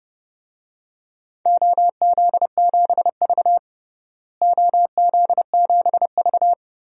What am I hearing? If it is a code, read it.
Morse code, heard as a keyed tone, about 30 wpm, 700 Hz. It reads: OZ7V OZ7V
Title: DXCC-100 Call Signs - 4 Characters - Repeated 3x - 30wpm